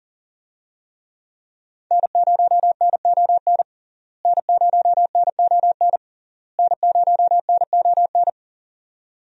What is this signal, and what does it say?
Morse code, heard as a keyed tone, about 40 wpm, 700 Hz. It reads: N0NON N0NON N0NON
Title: USA Call Signs - 5 Characters - Repeated 3x - 40wpm